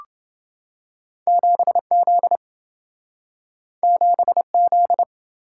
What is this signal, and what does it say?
Morse code, heard as a keyed tone, about 27 wpm, 700 Hz. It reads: 7Z 7Z